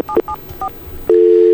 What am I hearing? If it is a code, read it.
DTMF, heard as telephone touch tones, tones of about 80 ms. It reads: 774